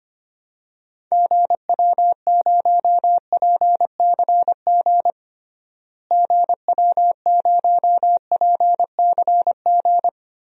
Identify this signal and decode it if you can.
Morse code, heard as a keyed tone, about 25 wpm, 700 Hz. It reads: GW0PCG GW0PCG
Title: DXCC-100 Call Signs - 6 Characters - Repeated 3x - 25wpm